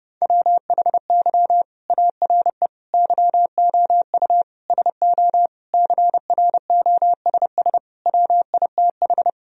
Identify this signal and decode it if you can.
Morse code, heard as a keyed tone, about 30 wpm, 700 Hz. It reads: WHY ARE YOU SO CROSS WITH
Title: Sentences from Top 1000 Words - Review Part 1 - 30wpm